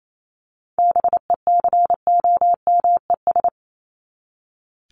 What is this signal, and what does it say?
Morse code, heard as a keyed tone, about 28 wpm, 700 Hz. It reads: BECOMES
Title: ES:S Suffix - Top 100 Words with Base Words 28wpm